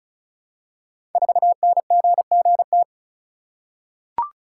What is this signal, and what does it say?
Morse code, heard as a keyed tone, about 36 wpm, 690 Hz. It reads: 4NGGT
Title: Code Groups - 5 Letter:Numbers - 35wpm